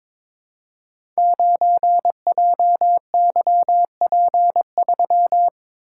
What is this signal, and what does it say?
Morse code, heard as a keyed tone, about 22 wpm, 700 Hz. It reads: 9JYP3
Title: Code Groups - 5 Letter:Numbers - 22wpm